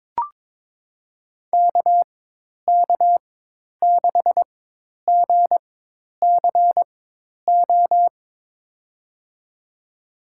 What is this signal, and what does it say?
Morse code, heard as a keyed tone, about 22 wpm, spaced as for 10 wpm, 700 Hz. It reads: KK6GCO
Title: USA Call Signs - 4x Character Spacing - 22wpm